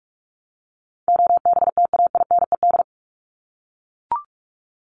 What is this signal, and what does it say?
Morse code, heard as a keyed tone, about 44 wpm, 700 Hz. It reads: OBTAINED
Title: ED Suffix - Top 100 Words with Base Words 45wpm